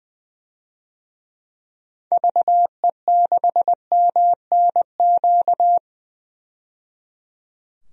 Morse code, heard as a keyed tone, about 21 wpm, 700 Hz. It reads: VE6MNQ
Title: DXCC-100 Call Signs - 6 Characters - 20wpm